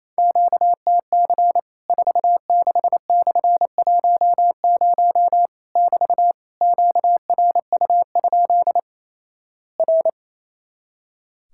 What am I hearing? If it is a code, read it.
Morse code, heard as a keyed tone, about 28 wpm, 700 Hz. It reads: QTC 46/10 <BT> QRU? R
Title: Worked All Europe DX Contest - QTC Exchange - 28wpm